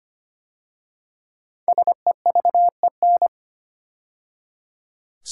Morse code, heard as a keyed tone, about 25 wpm, 700 Hz. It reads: SEVEN
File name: 5 Letter Words - First 3 Letters Given - 25wpm